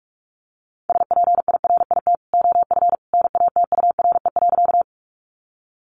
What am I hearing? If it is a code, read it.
Morse code, heard as a keyed tone, about 44 wpm, 700 Hz. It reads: SPIRIT OF NATURE.
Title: Sentences from Top 1400 Words 45wpm